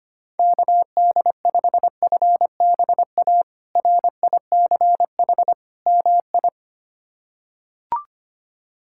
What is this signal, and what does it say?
Morse code, heard as a keyed tone, about 25 wpm, 700 Hz. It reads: KD5FBA RICH MI
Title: K1USN SST QSO Practice - 25wpm